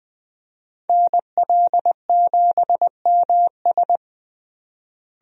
Morse code, heard as a keyed tone, about 20 wpm, 700 Hz. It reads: NL7MS
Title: USA Call Signs - 5 Characters - Repeated 3x - 20wpm